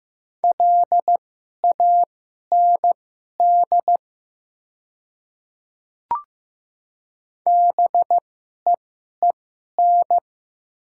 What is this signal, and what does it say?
Morse code, heard as a keyed tone, about 15 wpm, spaced as for 11 wpm, 700 Hz. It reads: LAND BEEN
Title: 2x Character Spacing - Top 200 Words - 15wpm